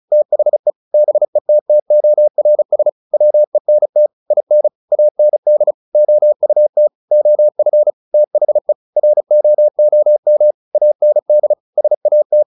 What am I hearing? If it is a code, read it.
Morse code, heard as a keyed tone, about 34 wpm, 600 Hz. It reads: THE BETTORS WENT IN AND OUT OF THE ROOM AND SAT
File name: the old man and the sea 2 35wpm